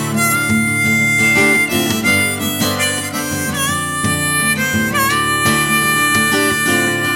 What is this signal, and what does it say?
Dial-up modem handshake.